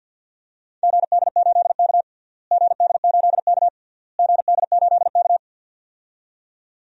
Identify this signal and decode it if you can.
Morse code, heard as a keyed tone, about 46 wpm, 700 Hz. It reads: GD8K GD8K GD8K
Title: DXCC-100 Call Signs - 4 Characters - Repeated 3x - 50wpm